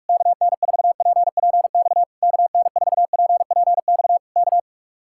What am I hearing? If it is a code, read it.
Morse code, heard as a keyed tone, about 44 wpm, 700 Hz. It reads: KN4PPX KN4PPX K